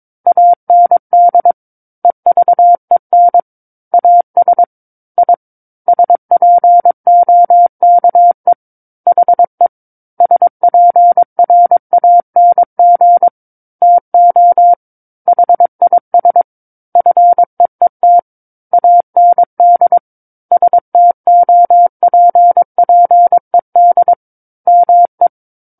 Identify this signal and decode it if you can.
Morse code, heard as a keyed tone, about 22 wpm, 700 Hz. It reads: AND EVEN AS I SPOKE HE SPRANG TO HIS FEET AND STOPPED ME
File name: Book One / 13-Chapter 13 - 22 WPM